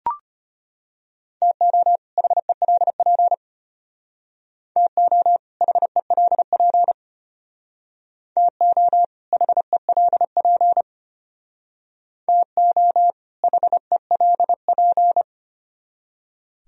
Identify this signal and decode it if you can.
Morse code, heard as a keyed tone, about 32 wpm, 700 Hz. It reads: TO HELP TO HELP TO HELP TO HELP